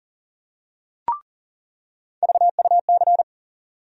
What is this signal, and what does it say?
Morse code, heard as a keyed tone, about 40 wpm, 700 Hz. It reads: VUC
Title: USA Call Sign Postfixes - 3 Letters 40wpm